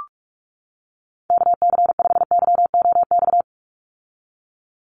Morse code, heard as a keyed tone, about 44 wpm, 700 Hz. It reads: KC5YOX
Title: USA Call Signs 45wpm